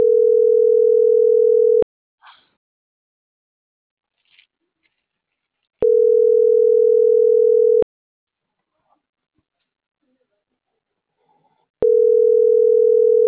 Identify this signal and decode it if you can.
Ringback tone (North America), 440 + 480 Hz.